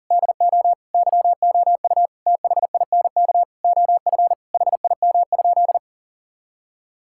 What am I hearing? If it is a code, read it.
Morse code, heard as a keyed tone, about 40 wpm, 700 Hz. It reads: DO YOU THINK OF HIM?
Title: Sentences from Top 1000 Words - Review Part 5 - 40wpm